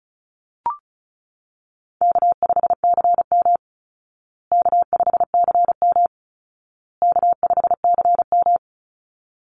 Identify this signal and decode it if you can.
Morse code, heard as a keyed tone, about 34 wpm, 700 Hz. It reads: K5CM K5CM K5CM
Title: Call Signs Repeated 2x - 35wpm